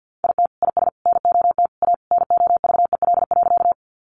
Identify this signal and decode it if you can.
Morse code, heard as a keyed tone, about 50 wpm, 700 Hz. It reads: IT IS NOT A NOVEL.
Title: Sentences from Top 1300 Words 50wpm